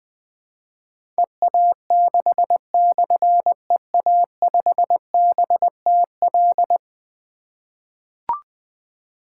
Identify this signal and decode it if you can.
Morse code, heard as a keyed tone, about 20 wpm, 700 Hz. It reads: EA6/EA5BTL
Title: Difficult Call Signs 20wpm